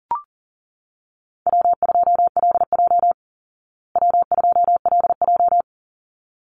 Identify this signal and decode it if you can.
Morse code, heard as a keyed tone, about 40 wpm, 700 Hz. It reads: W2LJ W2LJ
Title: Easy Call Signs Repeated 2x - 40wpm